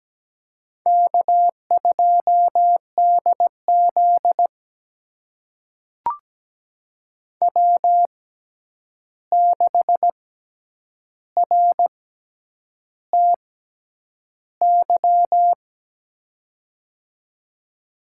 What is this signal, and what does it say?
Morse code, heard as a keyed tone, about 17 wpm, spaced as for 8 wpm, 700 Hz. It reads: K2DZ W6RTY
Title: USA Call Signs - 6x Character Spacing - 17wpm